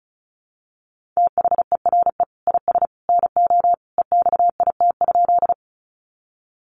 Morse code, heard as a keyed tone, about 34 wpm, 700 Hz. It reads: THERE IS NO EXIT?
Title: Sentences from Top 1100 Words 35wpm